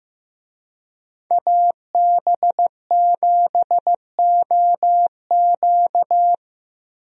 Morse code, heard as a keyed tone, about 15 wpm, 700 Hz. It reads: AB7OQ
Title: USA Call Signs - 5 Characters - Repeated 3x - 15wpm